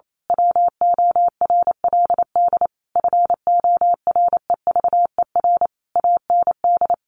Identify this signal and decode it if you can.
Morse code, heard as a keyed tone, about 28 wpm, 700 Hz. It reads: WORLD FOREVER AND
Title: A Sound Of Thunder 28wpm